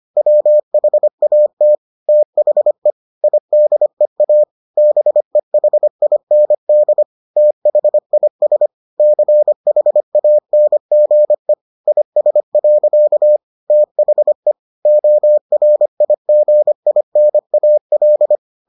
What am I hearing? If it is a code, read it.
Morse code, heard as a keyed tone, about 25 wpm, 600 Hz. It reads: WHAT THE IDEA BEHIND THIS CHANGE IS. THE ORIGINAL